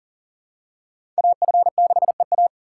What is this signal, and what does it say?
Morse code, heard as a keyed tone, about 39 wpm, 700 Hz. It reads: AF6EA
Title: USA Call Signs - 5 Characters - 2x Character Spacing - 40wpm